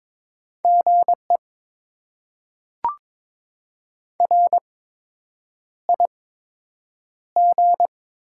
Morse code, heard as a keyed tone, about 23 wpm, spaced as for 8 wpm, 700 Hz. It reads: GE RIG
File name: Top 20 QSO Elements - 8x Character Spacing - 22wpm